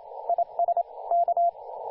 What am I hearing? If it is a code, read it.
Morse code, heard as a keyed tone, about 29 wpm, 690 Hz, with radio noise. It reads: IS K
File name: du1-ja1pbv-1-28-040mhz-cw-cqing